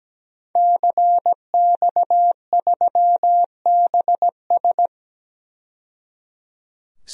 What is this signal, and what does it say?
Morse code, heard as a keyed tone, about 17 wpm, 700 Hz. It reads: CX3BS
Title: DXCC-100 Call Signs - 5 Characters - 17wpm